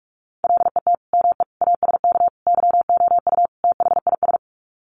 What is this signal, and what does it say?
Morse code, heard as a keyed tone, about 44 wpm, 700 Hz. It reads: LET ME ASK YOU THIS
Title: Sets of 5 Words 45wpm